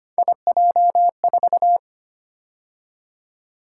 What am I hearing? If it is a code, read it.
Morse code, heard as a keyed tone, about 29 wpm, 700 Hz. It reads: IJ4
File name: Speed-Racing - Three Letter-Number Combinations - 25wpm